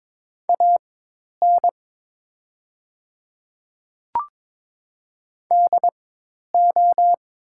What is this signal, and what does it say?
Morse code, heard as a keyed tone, about 22 wpm, spaced as for 10 wpm, 700 Hz. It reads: AN DO
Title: 4x Character Spacing - Top 200 Words - 22wpm